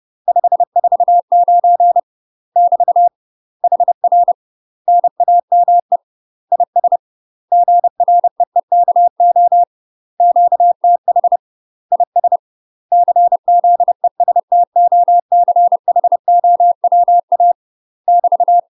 Morse code, heard as a keyed tone, about 30 wpm, 700 Hz. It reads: 549 <BT> HR NAME IS GREEKO QTH IS CZESTOCHOWA <BT>
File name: UK MASTER QSOs TRAINING / 30WPM / UK MASTER QSOs 0031